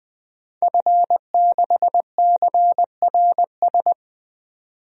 Morse code, heard as a keyed tone, about 20 wpm, 700 Hz. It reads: F6CRS
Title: DXCC-100 Call Signs - 5 Characters - 20wpm